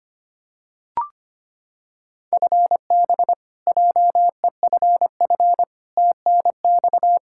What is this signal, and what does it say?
Morse code, heard as a keyed tone, about 25 wpm, 700 Hz. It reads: FB JEFF TNX
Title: Practice QSO - Exchange 3 - More Info - 25wpm